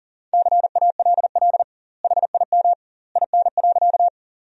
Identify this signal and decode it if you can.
Morse code, heard as a keyed tone, about 40 wpm, 700 Hz. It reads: CALL HIM IN.
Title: 2x Word-Spacing - Sentences from Top 100 Words - 40wpm